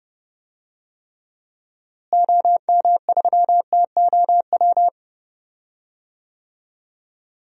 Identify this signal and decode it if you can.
Morse code, heard as a keyed tone, about 30 wpm, 700 Hz. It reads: OM3TOW
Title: DXCC-100 Call Signs - 6 Characters - Repeated 2x - 30wpm